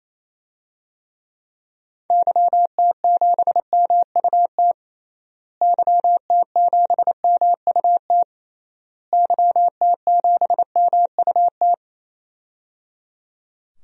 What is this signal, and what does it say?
Morse code, heard as a keyed tone, about 28 wpm, 700 Hz. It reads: YT7MUT YT7MUT YT7MUT
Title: DXCC-100 Call Signs - 6 Characters - Repeated 3x - 28wpm